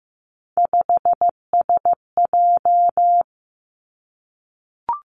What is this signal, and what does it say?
Morse code, heard as a keyed tone, about 15 wpm, 700 Hz. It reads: 5SJ
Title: Three Challenging Letter-Number Combinations 15wpm